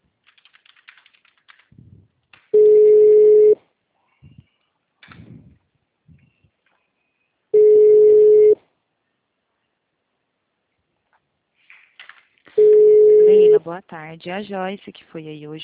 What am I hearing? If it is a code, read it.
Ringback tone (Europe, 425 Hz), with speech.